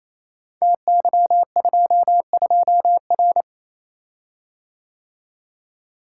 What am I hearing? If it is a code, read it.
Morse code, heard as a keyed tone, about 29 wpm, 700 Hz. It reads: TY22R
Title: Code Groups - 5 Letter:Numbers - 28wpm